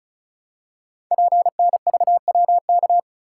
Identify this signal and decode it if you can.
Morse code, heard as a keyed tone, about 36 wpm, 700 Hz. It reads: PNVWK
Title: Code Groups - 5 Letters - 35wpm